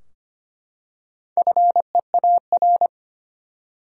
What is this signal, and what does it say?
Morse code, heard as a keyed tone, about 25 wpm, 700 Hz. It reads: FEAR
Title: Four Letter Words - 25wpm